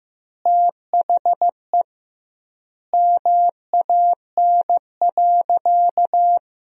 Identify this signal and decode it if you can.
Morse code, heard as a keyed tone, about 15 wpm, 700 Hz. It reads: THE MAN.
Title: 2x Word-Spacing - Sentences from Top 200 Words - 15wpm